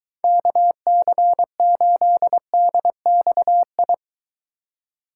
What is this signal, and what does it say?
Morse code, heard as a keyed tone, about 22 wpm, 700 Hz. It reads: KC8DXI